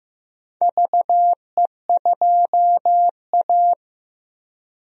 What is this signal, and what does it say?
Morse code, heard as a keyed tone, about 15 wpm, 700 Hz. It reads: VE2A
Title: DXCC-100 Call Signs - 4 Characters - Repeated 3x - 15wpm